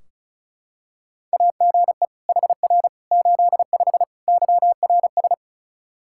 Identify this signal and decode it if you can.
Morse code, heard as a keyed tone, about 34 wpm, 700 Hz. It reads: AGE HR 85 YRS
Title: Practice QSO - Exchange 3 - More Info - 35wpm